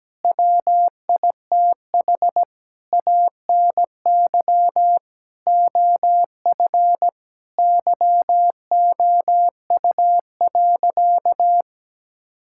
Morse code, heard as a keyed tone, about 17 wpm, 700 Hz. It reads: WITH ANY OF YOU.
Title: Sentences from Top 1600 Words 17wpm